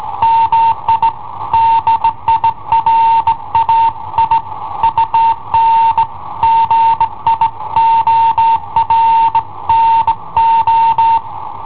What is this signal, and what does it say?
Morse code, heard as a keyed tone, about 17 wpm, 910 Hz. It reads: MI DIRAI UN GIORNO
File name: i7ohp / cassetta i7ohp B